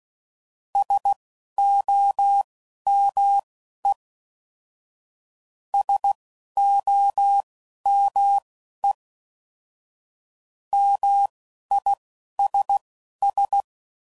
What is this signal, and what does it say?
Morse code, heard as a keyed tone, about 16 wpm, spaced as for 11 wpm, 800 Hz. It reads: SOME SOME MISS